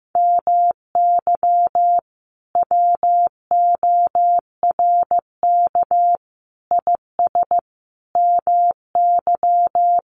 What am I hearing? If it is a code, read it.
Morse code, heard as a keyed tone, about 15 wpm, 700 Hz. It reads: MY WORK IS MY
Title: Sentences from Top 600 Words 15wpm